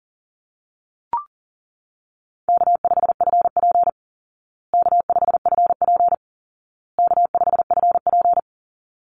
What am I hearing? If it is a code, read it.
Morse code, heard as a keyed tone, about 40 wpm, 700 Hz. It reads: K5FP K5FP K5FP
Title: Easy Call Signs Repeated 3x - 40wpm